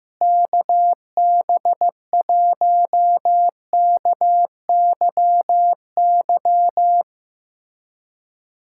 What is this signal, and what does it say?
Morse code, heard as a keyed tone, about 15 wpm, 700 Hz. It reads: KB1KYY
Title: USA Callsign DE Callsign - 6 Characters - 15wpm